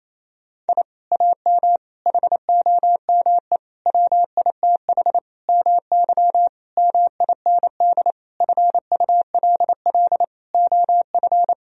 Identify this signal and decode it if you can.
Morse code, heard as a keyed tone, about 28 wpm, 700 Hz. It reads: I AM HOME WITH MY MIND FULL OF
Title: Sentences from Top 1000 Words - Review Part 3 - 28wpm